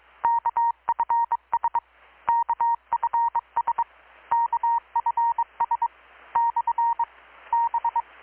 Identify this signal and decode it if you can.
Morse code, heard as a keyed tone, about 22 wpm, 930 Hz. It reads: KFS KFS KFS / B
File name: blog / swl / 6-abril-2019-12695-USB-KPS